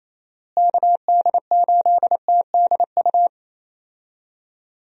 Morse code, heard as a keyed tone, about 28 wpm, 700 Hz. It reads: KD8TDU
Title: USA Call Signs - 6 Characters - Repeated 3x - 28wpm